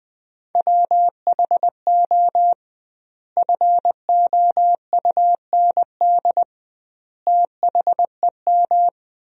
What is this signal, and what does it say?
Morse code, heard as a keyed tone, about 20 wpm, 700 Hz. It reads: WHO FOUND THEM
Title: 2x Word-Spacing - Sentences from Top 300 Words - 20wpm